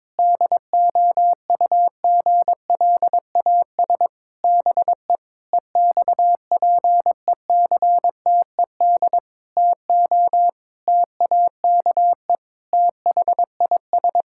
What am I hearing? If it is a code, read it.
Morse code, heard as a keyed tone, about 22 wpm, 700 Hz. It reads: DOUGLAS BE EXPECTED TO TAKE THIS
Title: Pangram Challenge v2 - 22wpm